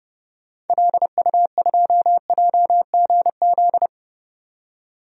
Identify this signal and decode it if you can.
Morse code, heard as a keyed tone, about 32 wpm, 700 Hz. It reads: LU2JGZ
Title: DXCC-100 Call Signs - 6 Characters - 30wpm